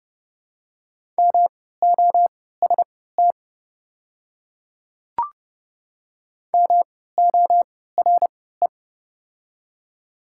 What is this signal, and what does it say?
Morse code, heard as a keyed tone, about 30 wpm, spaced as for 17 wpm, 700 Hz. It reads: MOST MORE
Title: 3x Character Spacing - Top 100 Words - 30wpm